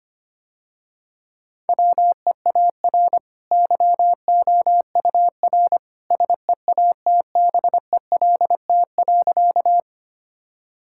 Morse code, heard as a keyed tone, about 25 wpm, 700 Hz. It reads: WEAR YOUR SEATBELT.
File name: Easy Sentences with Hints - 25wpm